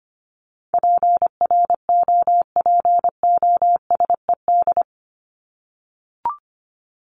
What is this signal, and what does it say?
Morse code, heard as a keyed tone, about 25 wpm, 700 Hz. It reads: PROPOSED
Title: ED Suffix - Top 100 Words with Base Words 25wpm